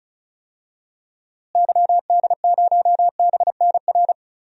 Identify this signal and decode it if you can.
Morse code, heard as a keyed tone, about 34 wpm, 700 Hz. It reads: YD0BNR
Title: DXCC-100 Call Signs - 6 Characters - Repeated 3x - 35wpm